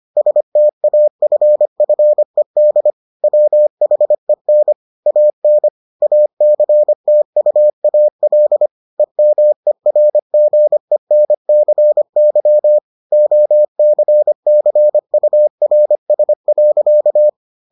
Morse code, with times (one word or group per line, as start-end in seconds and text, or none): 0.2-2.9 STAFFED
3.2-4.7 WHEN
5.1-5.7 AN
6.0-8.7 ACTUAL
9.0-12.8 EMERGENCY
13.1-17.3 OCCURS.